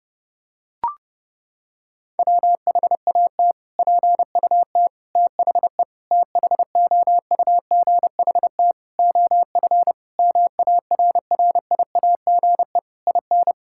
Morse code, heard as a keyed tone, about 30 wpm, 700 Hz. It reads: WHAT PUT THE THOUGHT OF MARRIAGE IN
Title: Sentences from Top 1000 Words - Review Part 5 - 30wpm